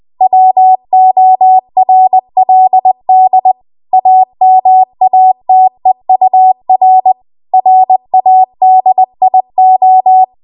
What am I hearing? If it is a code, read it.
Morse code, heard as a keyed tone, about 20 wpm, 750 Hz. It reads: WORLD AMATEUR RADIO
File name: Morse / 20WPM